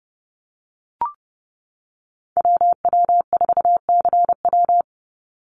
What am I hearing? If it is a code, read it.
Morse code, heard as a keyed tone, about 30 wpm, 700 Hz. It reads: WW4CW
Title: USA Call Signs 30wpm